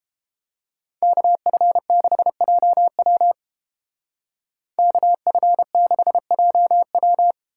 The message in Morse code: KF6JW KF6JW

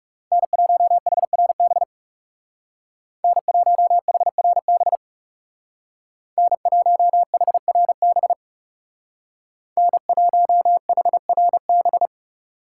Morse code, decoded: N1HRB N1HRB N1HRB N1HRB

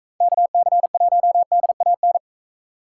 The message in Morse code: KC1DAN

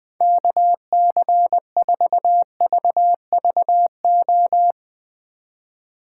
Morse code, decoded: KC4VVO

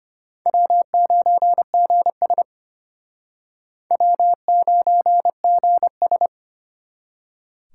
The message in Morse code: W9GS W9GS

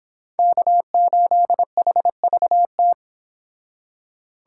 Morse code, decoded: K8HVT

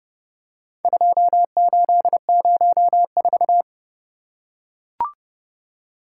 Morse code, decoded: 2804

